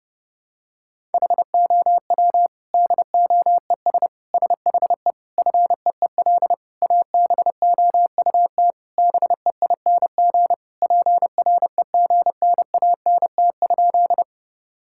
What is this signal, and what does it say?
Morse code, heard as a keyed tone, about 30 wpm, 700 Hz. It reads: HOW DOES SHE FEEL ABOUT BEING PREGNANT?
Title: Sentences from Top 1000 Words - Review Part 6 - 30wpm